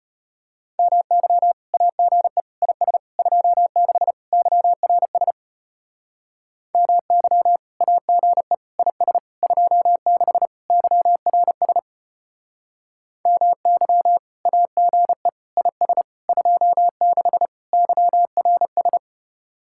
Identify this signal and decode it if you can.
Morse code, heard as a keyed tone, about 33 wpm, 700 Hz. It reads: MY AGE IS 26 YRS MY AGE IS 26 YRS MY AGE IS 26 YRS